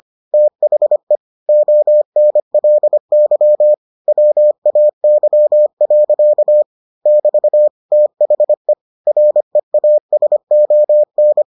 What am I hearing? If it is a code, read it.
Morse code, heard as a keyed tone, about 25 wpm, 600 Hz. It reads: THE ONLY WAY. <BT> THE REASON